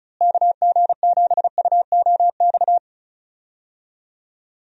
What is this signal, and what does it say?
Morse code, heard as a keyed tone, about 36 wpm, 700 Hz. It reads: KG7UOX
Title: USA Call Signs - 6 Characters - Repeated 2x - 35wpm